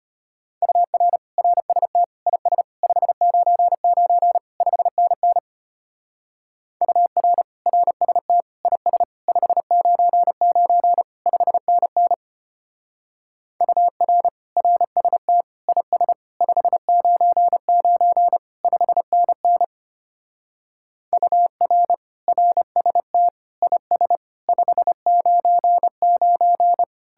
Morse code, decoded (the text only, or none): UR RST IS 599 5NN UR RST IS 599 5NN UR RST IS 599 5NN UR RST IS 599